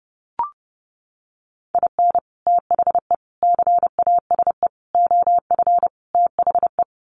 Morse code, heard as a keyed tone, about 30 wpm, 700 Hz. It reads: IN THE CASE OF THE
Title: Sets of 5 Words 30wpm